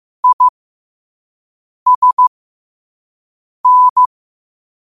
Morse code, heard as a keyed tone, about 15 wpm, spaced as for 5 wpm, 1000 Hz. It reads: ISN
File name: qotdfiles / 5wpm